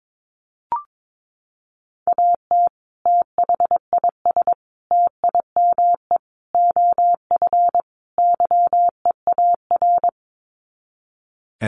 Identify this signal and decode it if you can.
Morse code, heard as a keyed tone, about 22 wpm, 700 Hz. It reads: AT THIS TIME OF YEAR
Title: Sets of 5 Words Encore2 22wpm